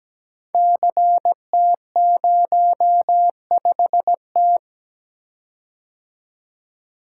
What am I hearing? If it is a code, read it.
Morse code, heard as a keyed tone, about 17 wpm, 700 Hz. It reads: CT05T
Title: Code Groups - 5 Letter:Numbers - 17wpm